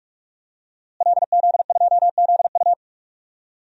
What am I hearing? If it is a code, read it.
Morse code, heard as a keyed tone, about 46 wpm, 700 Hz. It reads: LZ2ZU
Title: DXCC-100 Call Signs - 5 Characters - Repeated 2x - 45wpm